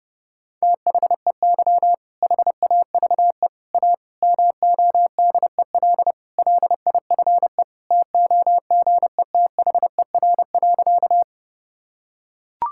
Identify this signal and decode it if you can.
Morse code, heard as a keyed tone, about 30 wpm, 700 Hz. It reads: THEY HAVE A MODEL LIFE TOGETHER.
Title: Sentences from Top 1000 Words - Review Part 5 - 30wpm